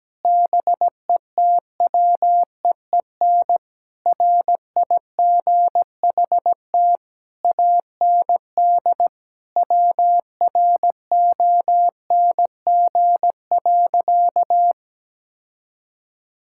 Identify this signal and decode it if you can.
Morse code, heard as a keyed tone, about 17 wpm, 700 Hz. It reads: BETWEEN RIGHT AND WRONG.
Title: Sentences from Top 1600 Words 17wpm